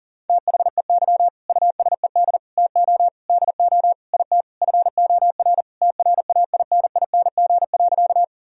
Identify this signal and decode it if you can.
Morse code, heard as a keyed tone, about 40 wpm, 700 Hz. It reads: THEY USED TO DO IT FOR TRAINING.
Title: Sentences from Top 1000 Words - Review Part 4 - 40wpm